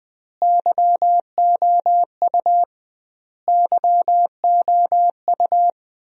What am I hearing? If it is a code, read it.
Morse code, heard as a keyed tone, about 20 wpm, 700 Hz. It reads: YOU YOU
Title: 2x Word-Spacing - Sentences from Top 400 Words - 20wpm